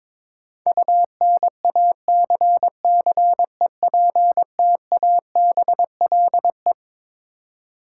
Morse code, heard as a keyed tone, about 21 wpm, 700 Hz. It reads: UNACCEPTABLE
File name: Twelve Letter Words - 22wpm